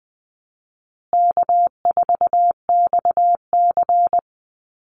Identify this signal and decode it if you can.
Morse code, heard as a keyed tone, about 20 wpm, 700 Hz. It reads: K4XC
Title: USA Call Signs Repeated 3x 20wpm